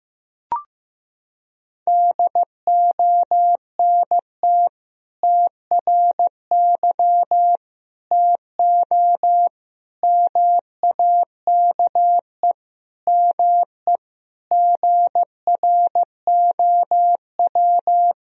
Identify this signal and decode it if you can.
Morse code, heard as a keyed tone, about 15 wpm, 700 Hz. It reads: DONT TRY TO MAKE ME GROW
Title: Sentences from Top 1000 Words - Review Part 1 - 15wpm